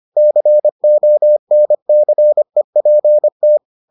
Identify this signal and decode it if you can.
Morse code, heard as a keyed tone, about 25 wpm, 600 Hz. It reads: CONCEPT